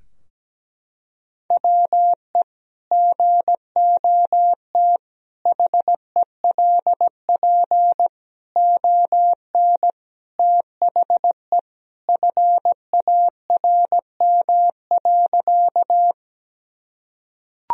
Morse code, heard as a keyed tone, about 17 wpm, 700 Hz. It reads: WE GOT HELP ON THE FARM.
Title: Sentences from Top 1000 Words - Review Part 2 - 17wpm